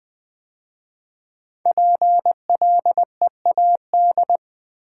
Morse code, heard as a keyed tone, about 20 wpm, 700 Hz. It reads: PLEAD
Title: 5 Letter Words - First 3 Letters Given - 20wpm